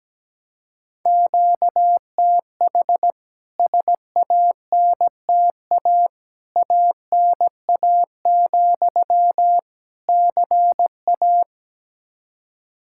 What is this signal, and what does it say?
Morse code, heard as a keyed tone, about 17 wpm, 700 Hz. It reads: QTH SANTA ANA, CA
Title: K1USN SST QSO Practice 17wpm